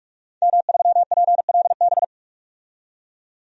Morse code, heard as a keyed tone, about 44 wpm, 700 Hz. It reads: M3PLB
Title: DXCC-100 Call Signs - 5 Characters - 45wpm